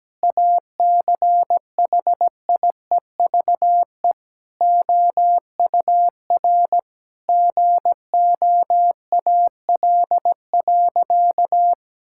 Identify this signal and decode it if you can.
Morse code, heard as a keyed tone, about 17 wpm, 700 Hz. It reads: ACHIEVE OUR GOAL.
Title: Sentences from Top 1000 Words - Review Part 6 - 17wpm